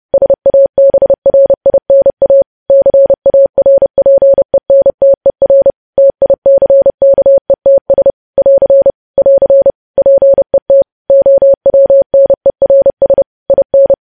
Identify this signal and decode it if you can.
Morse code, heard as a keyed tone, about 30 wpm, 550 Hz. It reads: SABRINA CARPENTER TICKETS <AR> <AR> PET OWNERS IN
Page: news headlines in morse code at 30 wpm